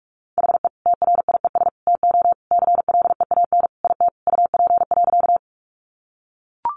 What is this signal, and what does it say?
Morse code, heard as a keyed tone, about 44 wpm, 700 Hz. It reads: HE TRIES TO CLEAN IT UP.
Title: Additional Easy Phrases 45wpm